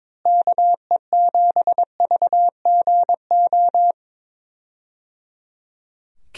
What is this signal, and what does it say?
Morse code, heard as a keyed tone, about 22 wpm, 700 Hz. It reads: KE7VGO
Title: USA Call Signs - 6 Characters - Repeated 3x - 22wpm